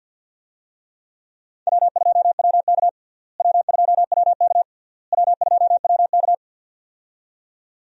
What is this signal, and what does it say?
Morse code, heard as a keyed tone, about 51 wpm, 700 Hz. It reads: W2WK W2WK W2WK
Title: USA Call Signs - 4 Characters - Repeated 3x - 50wpm